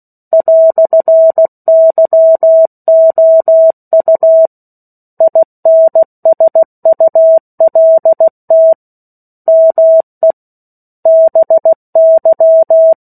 Morse code, heard as a keyed tone, about 16 wpm, 650 Hz. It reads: "YOU INSULT ME BY
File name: 03-Chapter-03 16 WPM